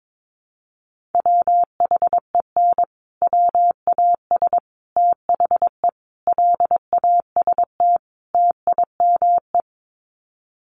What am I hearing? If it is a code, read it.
Morse code, heard as a keyed tone, about 22 wpm, 700 Hz. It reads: WHEN WAS THE LAST TIME
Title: Sets of 5 Words Encore 22wpm